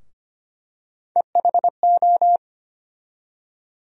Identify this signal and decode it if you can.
Morse code, heard as a keyed tone, about 25 wpm, 700 Hz. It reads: EHO
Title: USA Call Sign Postfixes - 3 Letters 25wpm